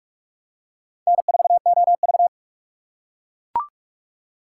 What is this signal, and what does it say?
Morse code, heard as a keyed tone, about 40 wpm, 700 Hz. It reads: N4OV